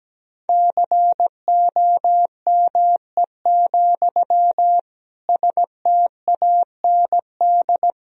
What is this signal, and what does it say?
Morse code, heard as a keyed tone, about 17 wpm, 700 Hz. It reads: COME, STAND
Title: Sentences from Top 1000 Words - Review Part 1 - 17wpm